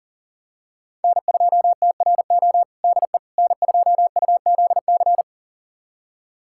Morse code, heard as a keyed tone, about 40 wpm, 700 Hz. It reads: N2TRO DE N2UZC
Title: USA Callsign DE Callsign - 5 Characters - 40wpm